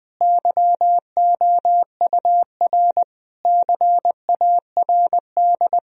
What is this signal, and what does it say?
Morse code, heard as a keyed tone, about 20 wpm, 700 Hz. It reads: YOUR CARD
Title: Sentences from Top 1000 Words - Review Part 4 - 20wpm